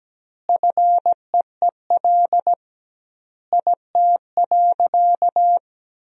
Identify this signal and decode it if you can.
Morse code, heard as a keyed tone, about 17 wpm, 700 Hz. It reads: FEEL IT.
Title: 2x Word-Spacing - Sentences from Top 400 Words - 17wpm